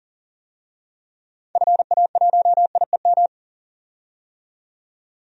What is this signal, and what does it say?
Morse code, heard as a keyed tone, about 39 wpm, 700 Hz. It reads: FA1IEM
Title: DXCC-100 Call Signs - 6 Characters - 40wpm